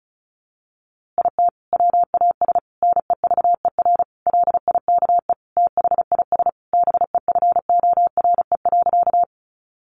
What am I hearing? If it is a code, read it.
Morse code, heard as a keyed tone, about 34 wpm, 700 Hz. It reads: IT WAS NEVER LIKE THIS BEFORE.
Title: A Sound Of Thunder 35wpm